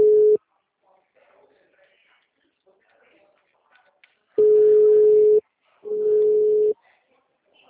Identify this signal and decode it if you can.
Ringback tone (Europe, 425 Hz), with speech.